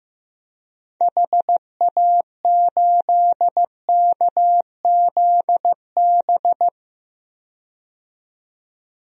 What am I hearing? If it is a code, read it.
Morse code, heard as a keyed tone, about 15 wpm, 700 Hz. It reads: HA8KZB
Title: DXCC-100 Call Signs - 6 Characters - 15wpm